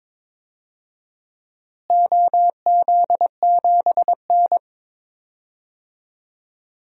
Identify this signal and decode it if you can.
Morse code, heard as a keyed tone, about 22 wpm, 700 Hz. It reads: OZ7N
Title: DXCC-100 Call Signs - 4 Characters - Repeated 3x - 22wpm